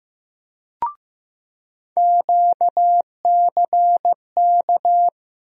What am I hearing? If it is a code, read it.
Morse code, heard as a keyed tone, about 16 wpm, 700 Hz. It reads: QCK
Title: USA Call Sign Postfixes - 3 Letters 15wpm